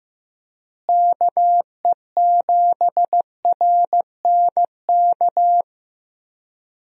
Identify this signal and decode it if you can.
Morse code, heard as a keyed tone, about 15 wpm, 700 Hz. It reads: KE7RNK
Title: USA Call Signs - 6 Characters - Repeated 2x - 15wpm